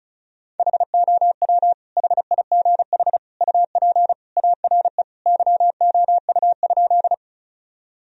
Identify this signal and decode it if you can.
Morse code, heard as a keyed tone, about 36 wpm, 700 Hz. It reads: HOW HIGH UP ARE YOU?
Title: Sentences from Top 1000 Words - Review Part 1 - 35wpm